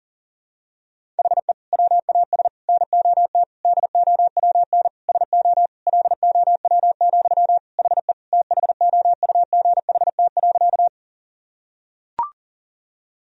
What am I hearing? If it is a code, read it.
Morse code, heard as a keyed tone, about 40 wpm, 700 Hz. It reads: HE WAS NOT DOWN SO LOW, HE THOUGHT.
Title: Sentences from Top 1000 Words - Review Part 1 - 40wpm